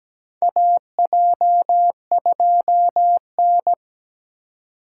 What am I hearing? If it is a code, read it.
Morse code, heard as a keyed tone, about 17 wpm, 700 Hz. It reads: AJ2N
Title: USA Call Signs - 4 Characters - 3x Character Spacing - 17wpm